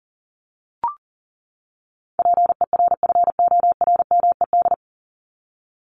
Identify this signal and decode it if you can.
Morse code, heard as a keyed tone, about 40 wpm, 700 Hz. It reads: PERFORMED
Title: ED Suffix - Top 100 Words with Base Words 40wpm